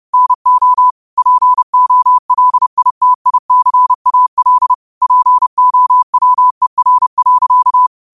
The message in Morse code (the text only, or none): NO POLITICAL POWER.